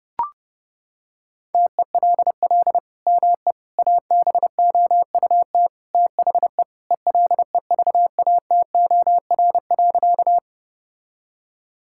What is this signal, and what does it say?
Morse code, heard as a keyed tone, about 32 wpm, 700 Hz. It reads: TELL ME ABOUT THE ELEVATOR.
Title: Sentences from Top 1600 Words 30wpm